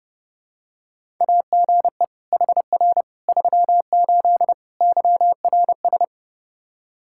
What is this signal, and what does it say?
Morse code, heard as a keyed tone, about 32 wpm, 700 Hz. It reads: AGE HR 38 YRS
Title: K1USN SST QSO Practice 30wpm